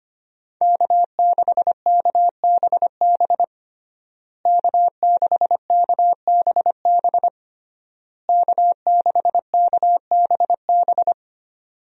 Morse code, heard as a keyed tone, about 25 wpm, 700 Hz. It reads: K6KBB K6KBB K6KBB
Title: USA Call Signs - 5 Characters - Repeated 3x - 25wpm